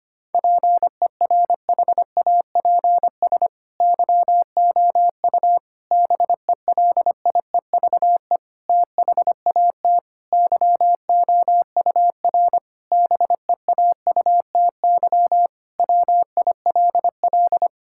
Morse code, heard as a keyed tone, about 25 wpm, 700 Hz. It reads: PERHAPS YOU BELIEVE THAT YOUR BEAUTY WILL
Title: Sentences from Top 1600 Words 25wpm